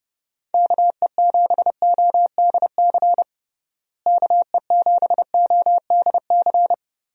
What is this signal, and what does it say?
Morse code, heard as a keyed tone, about 30 wpm, 700 Hz. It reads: KE7ODC KE7ODC
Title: USA Call Signs - 6 Characters - Repeated 3x - 30wpm